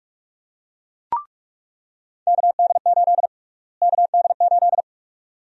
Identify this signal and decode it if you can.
Morse code, heard as a keyed tone, about 46 wpm, 700 Hz. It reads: KD8 KD8
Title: USA Call Sign Prefixes - Repeated 2x - 45wpm